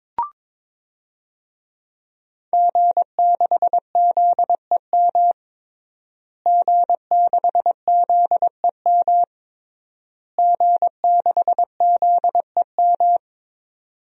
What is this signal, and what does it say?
Morse code, heard as a keyed tone, about 22 wpm, 700 Hz. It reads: G6ZEM G6ZEM G6ZEM
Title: DXCC-100 Call Signs - 5 Characters - Repeated 3x - 22wpm